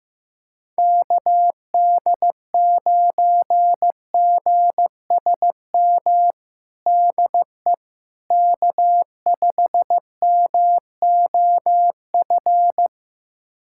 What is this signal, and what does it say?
Morse code, heard as a keyed tone, about 15 wpm, 700 Hz. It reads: KD9GSM DE K5MOF
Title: USA Callsign DE Callsign 15wpm